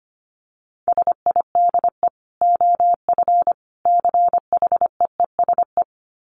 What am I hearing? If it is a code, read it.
Morse code, heard as a keyed tone, about 25 wpm, 700 Hz. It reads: SIDE OF CHEESE
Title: Dit Challenge 25wpm